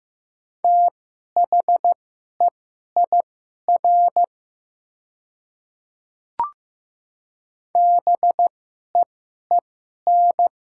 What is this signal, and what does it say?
Morse code, heard as a keyed tone, about 15 wpm, spaced as for 11 wpm, 690 Hz. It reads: THEIR BEEN
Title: 2x Character Spacing - Top 100 Words - 15wpm